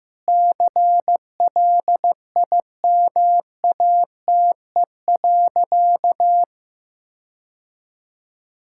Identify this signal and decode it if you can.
Morse code, heard as a keyed tone, about 14 wpm, 700 Hz. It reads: CLIMATE.
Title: Sentences from Top 1600 Words 15wpm